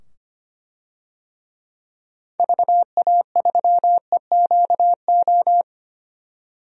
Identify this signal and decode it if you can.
Morse code, heard as a keyed tone, about 25 wpm, 700 Hz. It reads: VA3EQO
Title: DXCC-100 Call Signs - 6 Characters - 25wpm